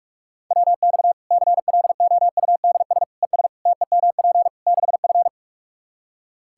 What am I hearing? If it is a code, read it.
Morse code, heard as a keyed tone, about 41 wpm, 700 Hz. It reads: WX CLOUDS ES TEMP 6F
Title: Practice QSO - Exchange 2 - Info - 45wpm